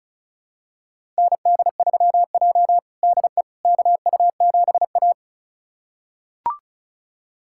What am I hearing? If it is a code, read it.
Morse code, heard as a keyed tone, about 36 wpm, 700 Hz. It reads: ND3J DE KU7A
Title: USA Callsign DE Callsign - 4 Characters - 35wpm